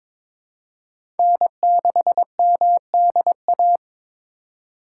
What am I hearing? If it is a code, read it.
Morse code, heard as a keyed tone, about 21 wpm, 700 Hz. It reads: N6MDA
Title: USA Call Signs - 5 Characters - 3x Character Spacing - 22wpm